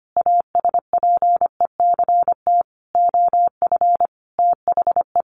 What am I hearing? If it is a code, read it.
Morse code, heard as a keyed tone, about 25 wpm, 700 Hz. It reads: ASPECT OF THE
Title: Sentences from Top 1300 Words 25wpm